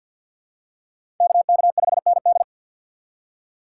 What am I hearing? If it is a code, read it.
Morse code, heard as a keyed tone, about 52 wpm, 700 Hz. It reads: KK5ND
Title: USA Call Signs - 3x Character Spacing - 50wpm